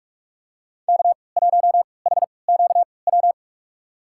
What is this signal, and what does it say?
Morse code, heard as a keyed tone, about 46 wpm, spaced as for 26 wpm, 700 Hz. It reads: K1HQW
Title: USA Call Signs - 5 Characters - 3x Character Spacing - 45wpm